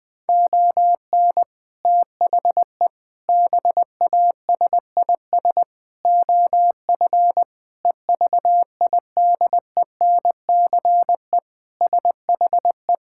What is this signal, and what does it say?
Morse code, heard as a keyed tone, about 20 wpm, 700 Hz. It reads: ON THE BASIS OF EVIDENCE SHE
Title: Sentences with 4-Word Phrases - 20wpm